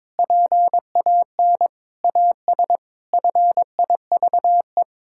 Morse code, heard as a keyed tone, about 22 wpm, 700 Hz. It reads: PAN AS FIVE
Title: Pangram Challenge v2 - 22wpm